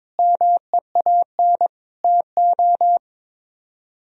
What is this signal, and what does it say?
Morse code, heard as a keyed tone, about 22 wpm, 700 Hz. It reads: MEAN TO
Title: Speed-Racing - Sets of 5 Words 22wpm